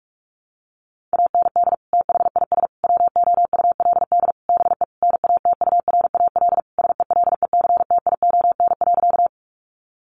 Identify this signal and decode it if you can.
Morse code, heard as a keyed tone, about 44 wpm, 700 Hz. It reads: AND THIS WOULD BE NATURAL SELECTION.
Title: Sentences from Top 1400 Words 45wpm